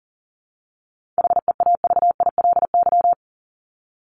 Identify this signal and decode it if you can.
Morse code, heard as a keyed tone, about 40 wpm, 700 Hz. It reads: HEAVILY
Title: LY Suffix - Top 100 Words with Base Words 40wpm